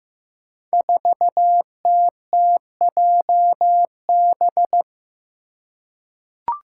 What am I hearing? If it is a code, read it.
Morse code, heard as a keyed tone, about 15 wpm, 700 Hz. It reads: 4TTJB
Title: Five Challenging Letter-Number Combinations 15wpm